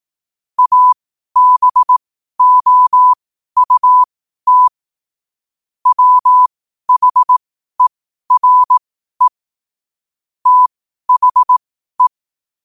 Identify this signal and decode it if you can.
Morse code, heard as a keyed tone, about 18 wpm, spaced as for 12 wpm, 1000 Hz. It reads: ABOUT WHERE THE
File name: qotdfiles / 12wpm